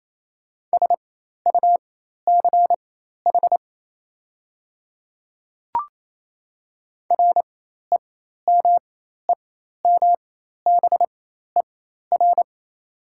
Morse code, heard as a keyed tone, about 28 wpm, spaced as for 13 wpm, 700 Hz. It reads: SUCH REMEMBER
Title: 4x Character Spacing - Top 500 Words - 28wpm